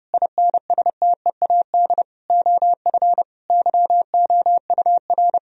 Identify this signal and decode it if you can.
Morse code, heard as a keyed tone, about 32 wpm, 700 Hz. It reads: INSTEAD OF YOUR
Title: Sentences from Top 1000 Words - Review Part 4 - 30wpm